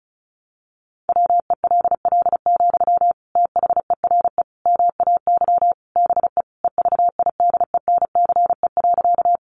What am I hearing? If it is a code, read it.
Morse code, heard as a keyed tone, about 34 wpm, 700 Hz. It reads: WELL, THERE MAY BE EVIDENCE.
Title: Sentences from Top 1300 Words 35wpm